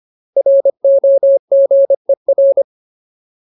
Morse code, heard as a keyed tone, about 25 wpm, 550 Hz. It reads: ROGER
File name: Ultimate Word List / 25WPM Ultimate